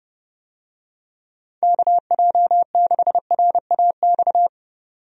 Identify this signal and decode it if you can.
Morse code, heard as a keyed tone, about 30 wpm, 700 Hz. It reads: KJ6RAX
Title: USA Call Signs - 6 Characters - 30wpm